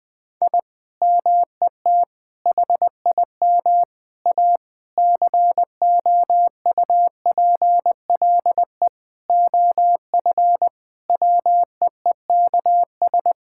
Morse code, heard as a keyed tone, about 20 wpm, 700 Hz. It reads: I MET HIM A COUPLE OF WEEKS
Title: Sentences with 5-Word Phrases - Spoken Beginning - 20wpm